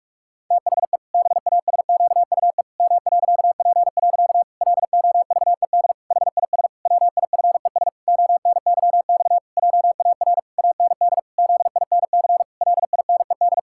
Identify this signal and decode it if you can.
Morse code, heard as a keyed tone, about 46 wpm, 700 Hz. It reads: THE BASQUE M.P. LOVED HIS WIFES ONYX JAR AND ZINC LINED
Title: Pangram Challenge v2 - 45wpm